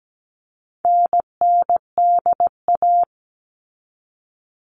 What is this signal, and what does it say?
Morse code, heard as a keyed tone, about 17 wpm, 700 Hz. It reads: NNDA
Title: All Cut Numbers - 4-Digit 17wpm